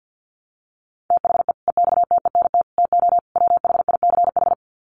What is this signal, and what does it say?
Morse code, heard as a keyed tone, about 48 wpm, 700 Hz. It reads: THE EXTENT TO WHICH
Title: Sets of 4 Words 50wpm